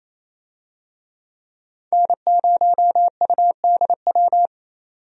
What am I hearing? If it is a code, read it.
Morse code, heard as a keyed tone, about 28 wpm, 700 Hz. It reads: N0UDW
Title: USA Call Signs - 5 Characters - 28wpm